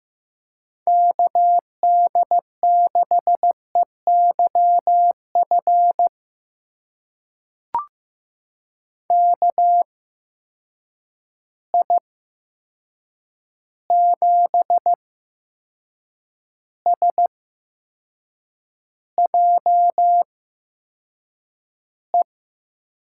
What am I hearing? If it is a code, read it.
Morse code, heard as a keyed tone, about 15 wpm, spaced as for 6 wpm, 700 Hz. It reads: KD6EYF KI7SJE
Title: USA Call Signs - 6 Characters - 8x Character Spacing - 15wpm